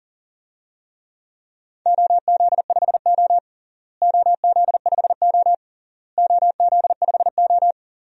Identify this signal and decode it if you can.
Morse code, heard as a keyed tone, about 40 wpm, 700 Hz. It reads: OZ5O OZ5O OZ5O
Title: DXCC-100 Call Signs - 4 Characters - Repeated 3x - 40wpm